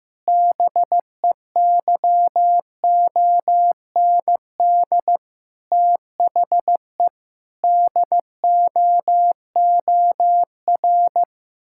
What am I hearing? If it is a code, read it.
Morse code, heard as a keyed tone, about 15 wpm, 700 Hz. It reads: BEYOND THE DOOR
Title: Sentences from Top 1000 Words - Review Part 6 - 15wpm